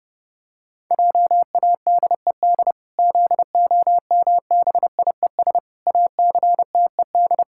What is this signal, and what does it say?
Morse code, heard as a keyed tone, about 30 wpm, 700 Hz. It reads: JADED ZOMBIES ACTED
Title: Pangram Challenge v2 - 30wpm